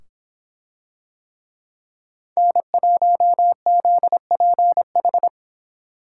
Morse code, heard as a keyed tone, about 24 wpm, 700 Hz. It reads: N1ZPH